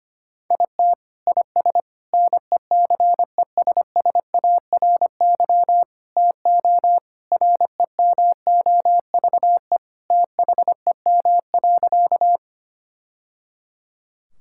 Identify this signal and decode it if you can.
Morse code, heard as a keyed tone, about 25 wpm, 700 Hz. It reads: IT IS NECESSARY TO REMOVE THEM.
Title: Sentences from Top 1000 Words - Review Part 5 - 25wpm